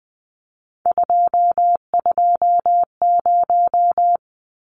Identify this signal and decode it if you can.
Morse code, heard as a keyed tone, about 20 wpm, 700 Hz. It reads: 220
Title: Numbers - 3-Digit 20wpm